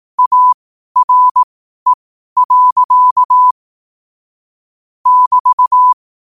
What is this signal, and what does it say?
Morse code, heard as a keyed tone, about 18 wpm, spaced as for 12 wpm, 1000 Hz. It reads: ARE. <BT>
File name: 12wpm